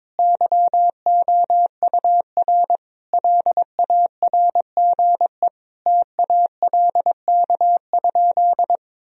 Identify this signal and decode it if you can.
Morse code, heard as a keyed tone, about 22 wpm, 700 Hz. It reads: YOUR LARGE TALK?
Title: Sentences from Top 1000 Words - Review Part 1 - 22wpm